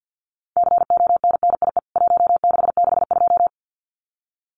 Morse code, heard as a keyed tone, about 49 wpm, 700 Hz. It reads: CONNIE 1662
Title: CWT Exchange 50wpm